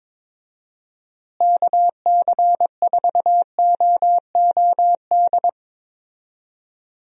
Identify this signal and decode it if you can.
Morse code, heard as a keyed tone, about 22 wpm, 700 Hz. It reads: KC4OOD
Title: USA Call Signs - 6 Characters - Repeated 3x - 22wpm